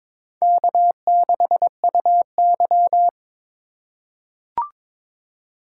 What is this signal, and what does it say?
Morse code, heard as a keyed tone, about 22 wpm, 700 Hz. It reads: K6UY
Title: USA Call Signs - 4 Characters - 4x Character Spacing - 22wpm